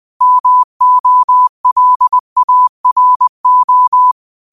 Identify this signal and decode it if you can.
Morse code, heard as a keyed tone, about 20 wpm, 1000 Hz. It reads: MOLARO
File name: qotdfiles / 20wpm